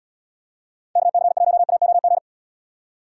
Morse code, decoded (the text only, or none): KC1AOG